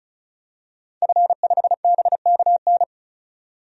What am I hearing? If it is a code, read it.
Morse code, heard as a keyed tone, about 34 wpm, 700 Hz. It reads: F5BKN
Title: DXCC-100 Call Signs - 5 Characters - 35wpm